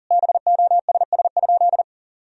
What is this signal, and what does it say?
Morse code, heard as a keyed tone, about 40 wpm, 700 Hz. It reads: BOSS?
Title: Sentences from Top 1000 Words - Review Part 4 - 40wpm